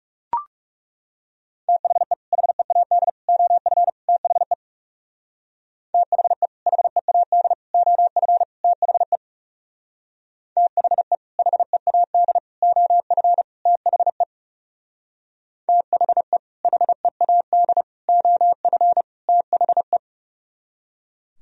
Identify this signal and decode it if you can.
Morse code, heard as a keyed tone, about 36 wpm, 700 Hz. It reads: THE HEAD OF THE THE HEAD OF THE THE HEAD OF THE THE HEAD OF THE
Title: Speed-Racing - Sets of 4 Words 30wpm